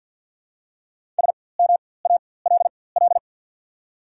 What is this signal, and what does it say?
Morse code, heard as a keyed tone, about 53 wpm, spaced as for 24 wpm, 700 Hz. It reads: SMALL
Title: 4x Character Spacing - Top 500 Words - 50wpm